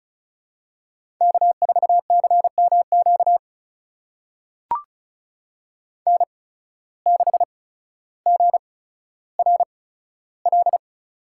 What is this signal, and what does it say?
Morse code, heard as a keyed tone, about 36 wpm, spaced as for 15 wpm, 700 Hz. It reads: K4CMQ N6GRL
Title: USA Call Signs - 8x Character Spacing - 35wpm